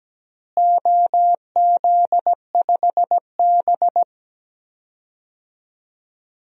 Morse code, heard as a keyed tone, about 17 wpm, 700 Hz. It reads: OZ5B
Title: DXCC-100 Call Signs - 4 Characters - Repeated 3x - 17wpm